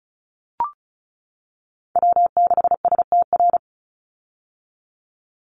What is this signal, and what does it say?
Morse code, heard as a keyed tone, about 36 wpm, 700 Hz. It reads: W6STR
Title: Call Signs - 35wpm